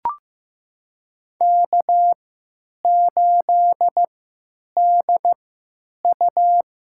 Morse code, heard as a keyed tone, about 15 wpm, spaced as for 9 wpm, 700 Hz. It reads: K8DU
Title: USA Call Signs - 3x Character Spacing - 15wpm